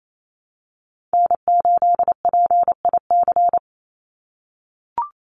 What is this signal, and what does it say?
Morse code, heard as a keyed tone, about 28 wpm, 700 Hz. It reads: N8PIC
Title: USA Call Signs 28wpm